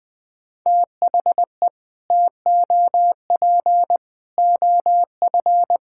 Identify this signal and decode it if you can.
Morse code, heard as a keyed tone, about 20 wpm, 700 Hz. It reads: THE TOP OF
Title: Build A Phrase - 3-Word Phrases - 20wpm